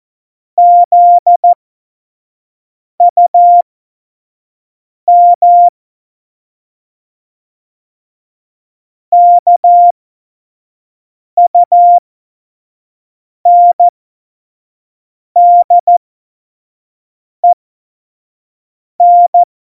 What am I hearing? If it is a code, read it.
Morse code, heard as a keyed tone, about 14 wpm, spaced as for 5 wpm, 700 Hz. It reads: ZUM KUNDEN
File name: 25 F70